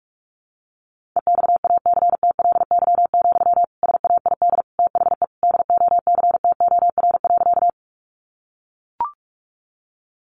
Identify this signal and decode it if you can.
Morse code, heard as a keyed tone, about 44 wpm, 700 Hz. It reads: EXACTLY, SAID THE DOCTOR.
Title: Sentences from Top 1500 Words 45wpm